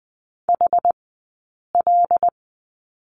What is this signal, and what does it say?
Morse code, heard as a keyed tone, about 20 wpm, spaced as for 8 wpm, 700 Hz. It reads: HL
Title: 204 Lesson 33d - Letter j - Two Characters 20wpm